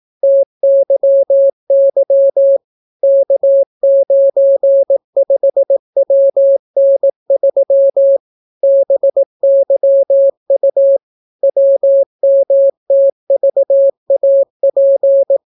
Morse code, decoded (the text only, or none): TYY K95WN3 BYU WMTVAP